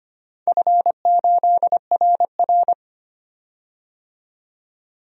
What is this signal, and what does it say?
Morse code, heard as a keyed tone, about 25 wpm, 700 Hz. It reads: F8RR
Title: DXCC-100 Call Signs - 4 Characters - 25wpm